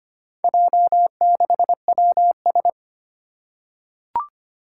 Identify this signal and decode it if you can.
Morse code, heard as a keyed tone, about 25 wpm, 700 Hz. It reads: J6WS
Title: Four Challenging Letter-Number Combinations 25wpm